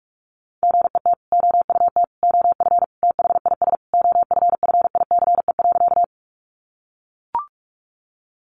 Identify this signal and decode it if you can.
Morse code, heard as a keyed tone, about 44 wpm, 700 Hz. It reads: GET OUT OF THIS OFFICE.
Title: Sentences from Top 1000 Words 45wpm